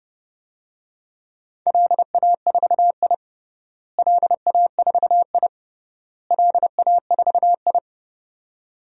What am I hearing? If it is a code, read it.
Morse code, heard as a keyed tone, about 32 wpm, 700 Hz. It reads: LA4I LA4I LA4I
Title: DXCC-100 Call Signs - 4 Characters - Repeated 3x - 30wpm